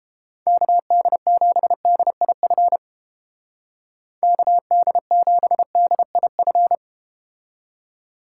Morse code, heard as a keyed tone, about 30 wpm, 700 Hz. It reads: KD7DIF KD7DIF